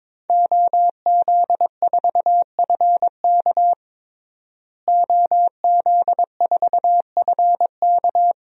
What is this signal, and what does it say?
Morse code, heard as a keyed tone, about 22 wpm, 700 Hz. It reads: OZ4FK OZ4FK
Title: DXCC-100 Call Signs - 5 Characters - Repeated 2x - 22wpm